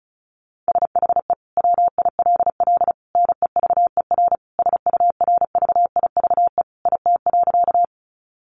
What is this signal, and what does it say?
Morse code, heard as a keyed tone, about 34 wpm, 700 Hz. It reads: SHE WILL NEVER SURVIVE IT.
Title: Sentences from Top 1300 Words 35wpm